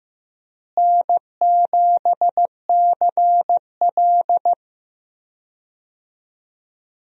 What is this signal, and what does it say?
Morse code, heard as a keyed tone, about 15 wpm, 700 Hz. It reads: N7CL
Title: USA Call Signs - 4 Characters - Repeated 3x - 15wpm